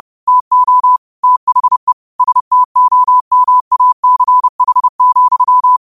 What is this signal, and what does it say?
Morse code, heard as a keyed tone, about 30 wpm, 1000 Hz. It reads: TO THE STOMACH,
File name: qotdfiles / 30wpm